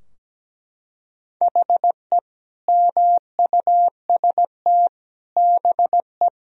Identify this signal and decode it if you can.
Morse code, heard as a keyed tone, about 17 wpm, 700 Hz. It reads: HE MUST BE
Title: Sentences from Top 1000 Words - Review Part 2 - 17wpm